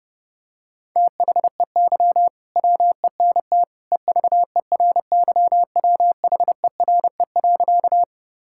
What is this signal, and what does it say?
Morse code, heard as a keyed tone, about 30 wpm, 700 Hz. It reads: THEY WENT EVERYWHERE.
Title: Sentences from Top 1600 Words 30wpm